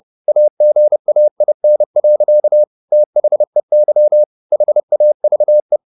Morse code, heard as a keyed tone, about 30 wpm, 600 Hz. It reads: AGAIN. THEY HAVE
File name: the old man and the sea / the old man and the sea 1 30wpm